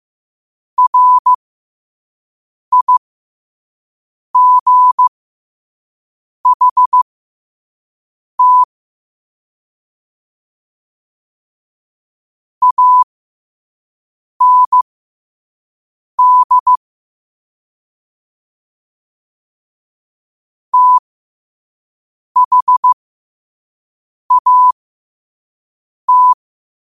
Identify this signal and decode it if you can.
Morse code, heard as a keyed tone, about 15 wpm, spaced as for 5 wpm, 1000 Hz. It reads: RIGHT AND THAT
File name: qotdfiles / 5wpm